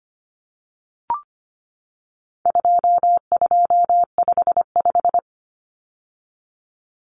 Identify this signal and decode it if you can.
Morse code, heard as a keyed tone, about 25 wpm, 700 Hz. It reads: 2255
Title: Numbers - 4-Digit 25wpm